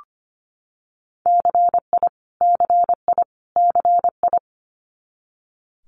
Morse code, heard as a keyed tone, about 25 wpm, 700 Hz. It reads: CI CI CI
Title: Multi-letter Phonemes Repeated 3x 25wpm